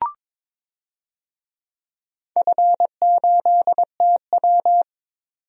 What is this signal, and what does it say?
Morse code, heard as a keyed tone, about 21 wpm, 690 Hz. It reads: F8TW
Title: DXCC-100 Call Signs - 4 Characters - 22wpm